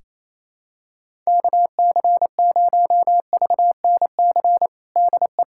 Morse code, heard as a keyed tone, about 28 wpm, 700 Hz. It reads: KC0VNC DE
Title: USA Callsign DE Callsign 28wpm